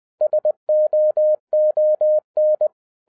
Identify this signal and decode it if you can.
Morse code, heard as a keyed tone, about 20 wpm, 610 Hz. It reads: SOON
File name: 2011-10-mp3 / n13-TIPS FROM MY 34 YEARS OF EXPERIENCE 20wpm